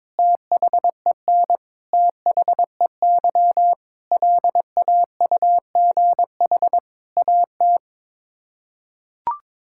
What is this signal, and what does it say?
Morse code, heard as a keyed tone, about 22 wpm, 700 Hz. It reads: THEN THEY LAUGH AT
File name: practice / 22wpm / Sets of 8 Words - 22wpm